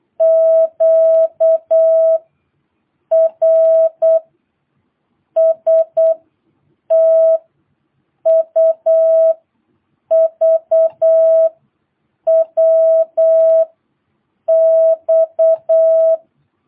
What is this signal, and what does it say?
Morse code, heard as a keyed tone, about 8 wpm, 650 Hz. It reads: Q R S T U V W X
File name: jh8cbh / 2022cw